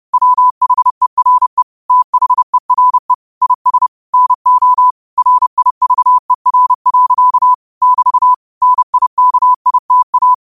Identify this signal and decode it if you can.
Morse code, heard as a keyed tone, about 30 wpm, 1000 Hz. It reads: WHERE THERE IS NO RIVER. <BT> NIKITA
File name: qotdfiles / 30wpm